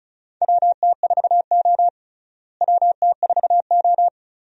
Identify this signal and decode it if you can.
Morse code, heard as a keyed tone, about 34 wpm, 700 Hz. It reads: WT4O WT4O
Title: USA Call Signs - 4 Characters - Repeated 2x - 35wpm